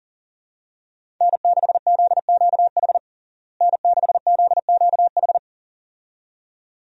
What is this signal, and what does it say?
Morse code, heard as a keyed tone, about 37 wpm, 700 Hz. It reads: N6ZQH N6ZQH
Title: USA Call Signs - 5 Characters - Repeated 2x - 40wpm